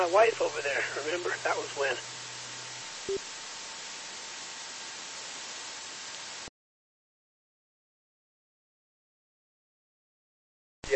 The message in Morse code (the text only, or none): none